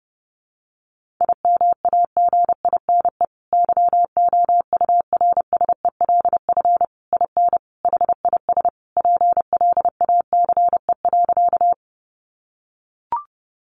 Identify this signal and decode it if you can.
Morse code, heard as a keyed tone, about 30 wpm, 700 Hz. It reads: IMAGINE YOURSELF IN HIS PLACE.
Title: Sentences from Top 1400 Words 30wpm